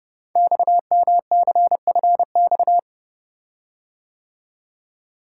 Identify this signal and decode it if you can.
Morse code, heard as a keyed tone, about 31 wpm, 690 Hz. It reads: XMCFX
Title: Code Groups - 5 Letters - 30wpm